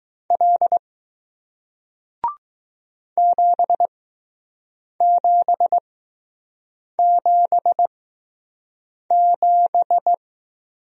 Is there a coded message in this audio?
Yes: Morse code from 0.3 to 10.1 s, heard as a keyed tone, about 20 wpm, spaced as for 6 wpm, 700 Hz. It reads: L 7777